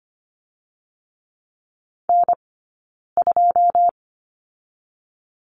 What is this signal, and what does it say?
Morse code, heard as a keyed tone, about 25 wpm, 700 Hz. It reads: N 2
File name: course / 25:10wpm / 099 Lesson 17d - Letter u - Two Characters 25wpm